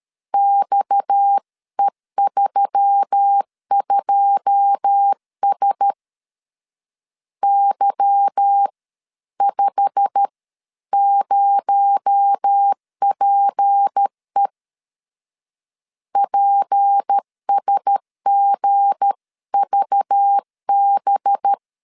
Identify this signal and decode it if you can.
Morse code, heard as a keyed tone, about 13 wpm, spaced as for 11 wpm, 800 Hz. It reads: XE32S Y50PE PSGVB
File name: CW / DET / 1 9 1500C S C 12wpm 3811